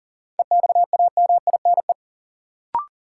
Morse code, heard as a keyed tone, about 40 wpm, 700 Hz. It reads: EXAMINE
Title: Sentences from Top 1600 Words - New Word Drill Alt - 40wpm